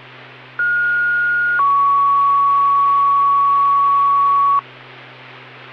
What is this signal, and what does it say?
Two-tone page, heard as a one-second tone, then a three-second tone, about 1430 then 1120 Hz. It is heard from 0.6 to 4.6 s.